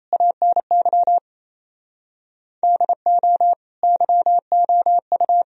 Morse code, heard as a keyed tone, about 35 wpm, 700 Hz. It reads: ANY DO YOU
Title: Speed-Racing - Sets of 4 Words 28wpm